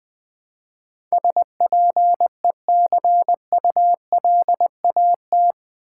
Morse code, heard as a keyed tone, about 20 wpm, 700 Hz. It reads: SPECULAT
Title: Build A Word - 4 Letters At A Time - 11-Letter Words - 20wpm